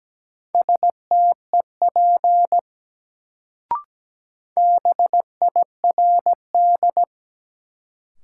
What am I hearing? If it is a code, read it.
Morse code, heard as a keyed tone, about 17 wpm, 700 Hz. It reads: STEP BIRD
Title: Sentences from Top 400 Words - New Word Drill - 17wpm